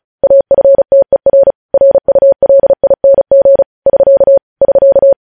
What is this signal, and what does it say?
Morse code, heard as a keyed tone, about 36 wpm, 550 Hz. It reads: AFTER RULING <SK> <SK>
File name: podcast / News headlines-1x-35x35